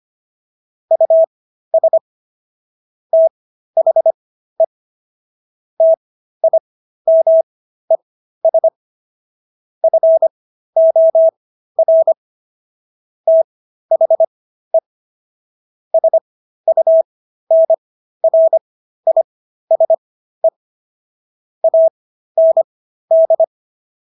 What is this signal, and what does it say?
Morse code, heard as a keyed tone, about 25 wpm, spaced as for 13 wpm, 650 Hz. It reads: US THE TIMES FOR THE SUNRISE AND